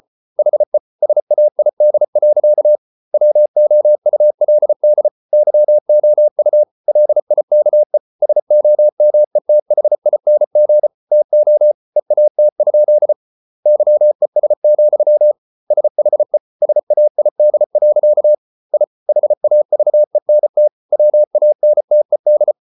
Morse code, heard as a keyed tone, about 34 wpm, 600 Hz. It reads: HE SAID. WOULD YOU LIKE SOMETHING TO EAT? YES, SHE SAID. I HAVENT WANTED